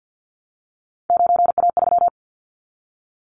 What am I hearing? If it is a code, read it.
Morse code, heard as a keyed tone, about 50 wpm, 700 Hz. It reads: 9A3
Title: Callsign Prefixes for Top 50 DXCC - 50wpm